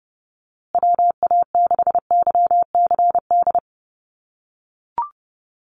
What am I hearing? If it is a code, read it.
Morse code, heard as a keyed tone, about 30 wpm, 700 Hz. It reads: WA6YCD
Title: USA Call Signs Repeated 2x 30wpm